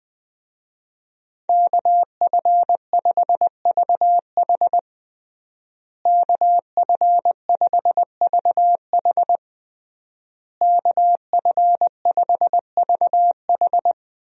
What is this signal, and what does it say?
Morse code, heard as a keyed tone, about 20 wpm, 700 Hz. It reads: KF5VH KF5VH KF5VH
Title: USA Call Signs - 5 Characters - Repeated 3x - 20wpm